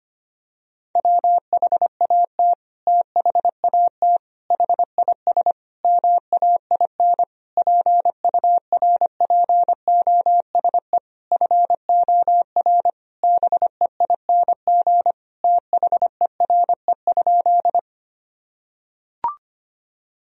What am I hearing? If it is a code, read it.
Morse code, heard as a keyed tone, about 25 wpm, 700 Hz. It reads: WHAT THAT HIS MAIN PURPOSE FOR BEING THERE?
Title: Sentences from Top 1000 Words - Review Part 2 - 25wpm